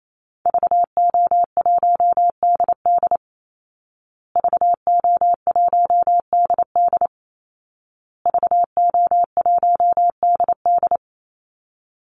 Morse code, heard as a keyed tone, about 28 wpm, 700 Hz. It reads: VO1DD VO1DD VO1DD
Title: Call Signs Repeated 2x - 28wpm